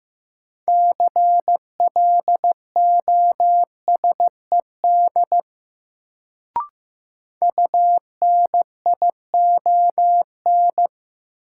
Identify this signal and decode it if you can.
Morse code, heard as a keyed tone, about 15 wpm, 700 Hz. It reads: CLOSED UNION
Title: Sentences from Top 1400 Words - New Word Drill - 15wpm